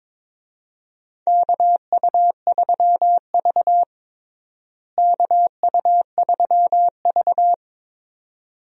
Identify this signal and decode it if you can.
Morse code, heard as a keyed tone, about 22 wpm, 700 Hz. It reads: KU3V KU3V
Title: USA Call Signs - 4 Characters - Repeated 2x - 22wpm